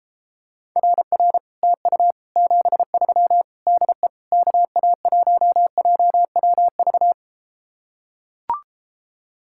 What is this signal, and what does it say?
Morse code, heard as a keyed tone, about 33 wpm, 700 Hz. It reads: RR TU 73 DE KA1JWV